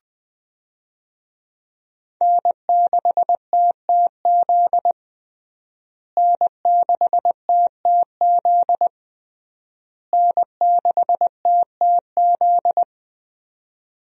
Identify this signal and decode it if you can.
Morse code, heard as a keyed tone, about 21 wpm, 700 Hz. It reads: N6TTZ N6TTZ N6TTZ
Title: USA Call Signs - 5 Characters - Repeated 3x - 20wpm